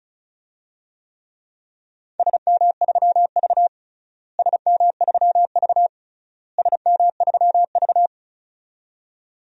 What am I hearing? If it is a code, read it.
Morse code, heard as a keyed tone, about 34 wpm, 700 Hz. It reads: SM3V SM3V SM3V
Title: DXCC-100 Call Signs - 4 Characters - Repeated 3x - 35wpm